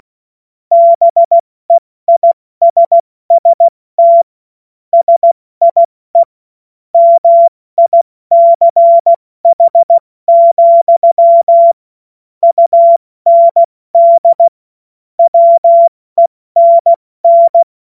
Morse code, heard as a keyed tone, about 16 wpm, 680 Hz. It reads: BEISST SIE MICH, UND WENN
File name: Witz-22-Juli-2025-70-F80